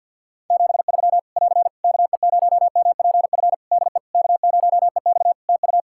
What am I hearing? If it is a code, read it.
Morse code, heard as a keyed tone, about 50 wpm, 700 Hz. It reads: 73 <AR> KE0MPF DE K0EX TU
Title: Practice QSO - Exchange 4 - Ending - 50wpm